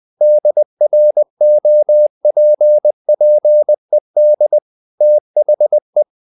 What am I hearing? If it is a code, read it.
Morse code, heard as a keyed tone, about 20 wpm, 600 Hz. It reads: DROPPED THE
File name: the old man and the sea part 2 20wpm